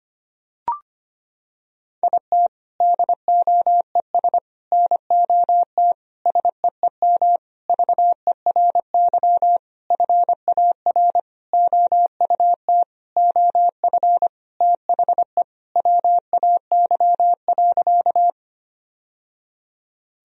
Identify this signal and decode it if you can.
Morse code, heard as a keyed tone, about 25 wpm, 700 Hz. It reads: IT DOES NOT SEEM VERY FAR OUT OF THE WAY.
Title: Sentences from Top 1000 Words - Review Part 1 - 25wpm